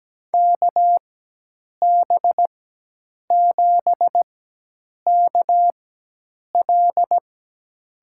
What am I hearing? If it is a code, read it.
Morse code, heard as a keyed tone, about 17 wpm, spaced as for 8 wpm, 700 Hz. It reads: KB7KL
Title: USA Call Signs - 4x Character Spacing - 17wpm